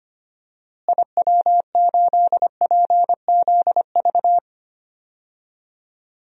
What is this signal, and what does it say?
Morse code, heard as a keyed tone, about 25 wpm, 700 Hz. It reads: IW8PZV
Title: DXCC-100 Call Signs - 6 Characters - Repeated 3x - 25wpm